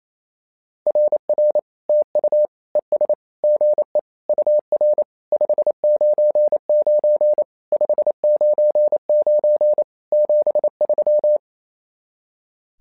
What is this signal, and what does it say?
Morse code, heard as a keyed tone, about 27 wpm, 600 Hz. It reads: RR TU ES GE UR 599 599 73
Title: SOTA QSO Practice 28wpm